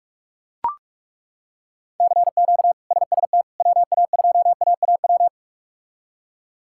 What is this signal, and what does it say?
Morse code, heard as a keyed tone, about 41 wpm, 700 Hz. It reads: CQ SST WA2AAW
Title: K1USN SST QSO Practice - 45wpm